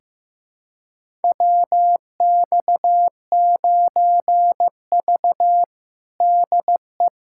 Morse code, heard as a keyed tone, about 15 wpm, 700 Hz. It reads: WX9V DE
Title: USA Callsign DE Callsign - 4 Characters - 15wpm